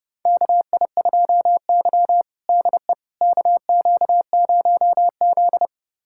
Morse code, heard as a keyed tone, about 30 wpm, 700 Hz. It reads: KI2Y DE KQ0Z
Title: USA Callsign DE Callsign - 4 Characters - 30wpm